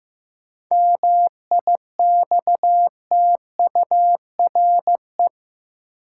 Morse code, heard as a keyed tone, about 16 wpm, 700 Hz. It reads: MIXTURE
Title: Sentences from Top 1500 Words - New Word Drill - 15wpm